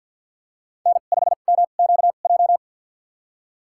Morse code, heard as a keyed tone, about 53 wpm, spaced as for 37 wpm, 700 Hz. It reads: N5MQJ
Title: USA Call Signs - 5 Characters - 2x Character Spacing - 50wpm